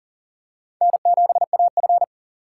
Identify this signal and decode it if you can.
Morse code, heard as a keyed tone, about 40 wpm, 700 Hz. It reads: N7AF
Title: USA Call Signs - 3x Character Spacing - 40wpm